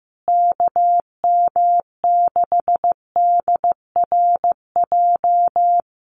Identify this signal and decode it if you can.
Morse code, heard as a keyed tone, about 15 wpm, 700 Hz. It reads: KM6DRJ
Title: USA Call Signs Repeated 3x 15wpm